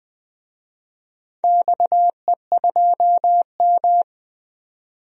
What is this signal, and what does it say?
Morse code, heard as a keyed tone, about 20 wpm, 700 Hz. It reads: XE2M
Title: DXCC-100 Call Signs - 4 Characters - 20wpm